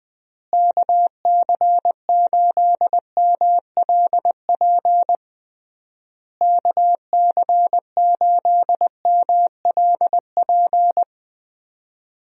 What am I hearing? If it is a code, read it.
Morse code, heard as a keyed tone, about 20 wpm, 700 Hz. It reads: KC8MLP KC8MLP
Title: USA Call Signs - 6 Characters - Repeated 3x - 20wpm